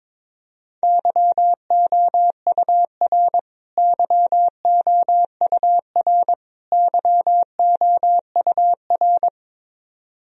Morse code, heard as a keyed tone, about 22 wpm, 700 Hz. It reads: YOUR YOUR YOUR
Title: Top 100 Words Repeated 22wpm